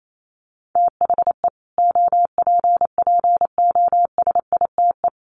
Morse code, heard as a keyed tone, about 28 wpm, 700 Hz. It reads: THE OPPOSITE
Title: Sets of 5 Words Encore 28wpm